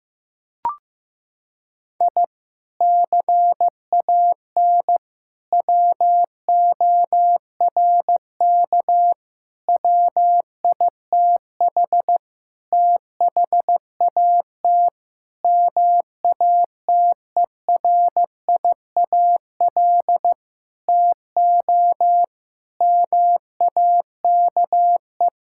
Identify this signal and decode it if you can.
Morse code, heard as a keyed tone, about 15 wpm, 700 Hz. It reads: I CAN WORK WITH THAT MATERIAL TO MAKE
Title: Sentences from Top 1000 Words - Review Part 5 - 15wpm